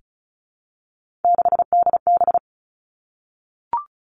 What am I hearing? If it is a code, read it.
Morse code, heard as a keyed tone, about 36 wpm, 700 Hz. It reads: 6DB
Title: Three Challenging Letter-Number Combinations 35wpm